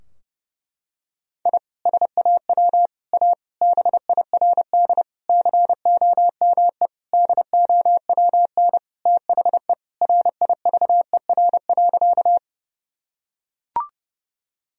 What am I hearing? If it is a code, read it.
Morse code, heard as a keyed tone, about 30 wpm, 700 Hz. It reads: I SAW A BIRD COME DOWN THE RIVER.
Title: Sentences from Top 1000 Words - Review Part 2 - 30wpm